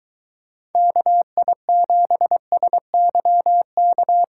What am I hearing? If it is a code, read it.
Morse code, heard as a keyed tone, about 23 wpm, 700 Hz. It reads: KI7SYK